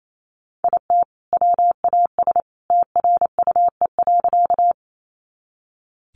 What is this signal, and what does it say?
Morse code, heard as a keyed tone, about 28 wpm, 700 Hz. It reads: IT WAS TRUE.
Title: Sentences from Top 400 Words 28wpm